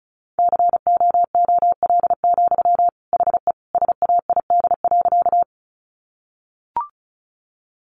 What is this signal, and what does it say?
Morse code, heard as a keyed tone, about 34 wpm, 700 Hz. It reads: COOL, HE SAID.
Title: Sentences from Top 1200 Words 35wpm